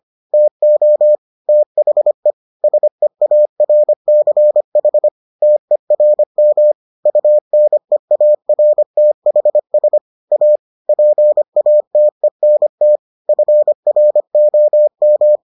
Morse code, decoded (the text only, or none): TO THE SEARCH TERM UNEARTHS A PATENT FROM